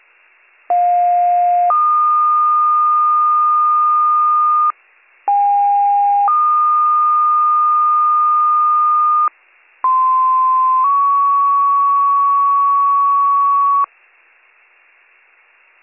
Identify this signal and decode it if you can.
Two-tone page, heard as a one-second tone, then a three-second tone, about 710 then 1220 Hz, about 810 then 1220 Hz, about 1040 then 1120 Hz. It is heard from 0.7 to 13.9 s.